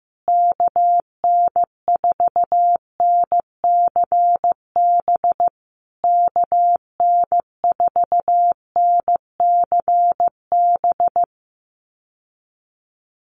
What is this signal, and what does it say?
Morse code, heard as a keyed tone, about 15 wpm, 700 Hz. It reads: KN4NCB KN4NCB
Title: USA Call Signs Repeated 3x 15wpm